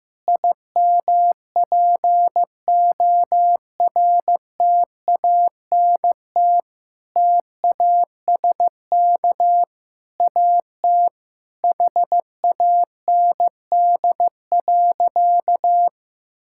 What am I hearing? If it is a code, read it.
Morse code, heard as a keyed tone, about 15 wpm, 700 Hz. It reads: IMPORTANT TASK AT HAND.
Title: Sentences from Top 1600 Words 15wpm